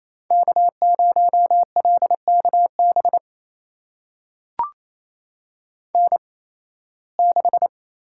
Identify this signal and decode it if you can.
Morse code, heard as a keyed tone, about 28 wpm, spaced as for 18 wpm, 700 Hz. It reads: K0LKB N6
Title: USA Call Signs - 5 Characters - 8x Character Spacing - 28wpm